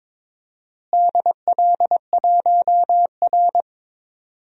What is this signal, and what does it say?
Morse code, heard as a keyed tone, about 22 wpm, 700 Hz. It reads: DL1R
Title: DXCC-100 Call Signs - 4 Characters - Repeated 3x - 22wpm